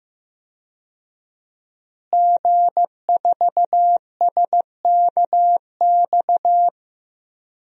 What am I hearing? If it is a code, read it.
Morse code, heard as a keyed tone, about 15 wpm, 700 Hz. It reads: G4SKX
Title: DXCC-100 Call Signs - 5 Characters - Repeated 2x - 15wpm